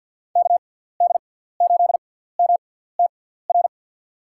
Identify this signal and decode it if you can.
Morse code, heard as a keyed tone, about 53 wpm, spaced as for 17 wpm, 700 Hz. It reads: KD8MTR
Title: USA Call Signs - 6 Characters - 6x Character Spacing - 50wpm